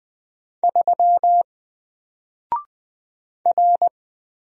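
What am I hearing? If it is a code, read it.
Morse code, heard as a keyed tone, about 20 wpm, 700 Hz. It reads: 3 R
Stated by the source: Single letter and number Morse drill at 20 wpm, from N4LCD.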